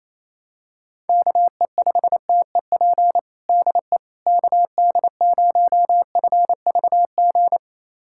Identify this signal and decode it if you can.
Morse code, heard as a keyed tone, about 28 wpm, 700 Hz. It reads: KE5TEP DE KD0FVG
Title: USA Callsign DE Callsign - 6 Characters - 28wpm